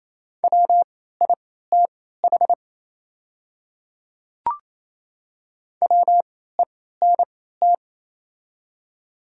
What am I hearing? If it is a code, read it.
Morse code, heard as a keyed tone, about 28 wpm, spaced as for 16 wpm, 700 Hz. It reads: WITH WENT
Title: 3x Character Spacing - Top 200 Words - 28wpm